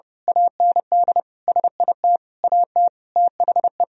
Morse code, heard as a keyed tone, about 30 wpm, 700 Hz. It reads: AND SIT AT THE
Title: Sentences from Top 1000 Words - Review Part 2 - 30wpm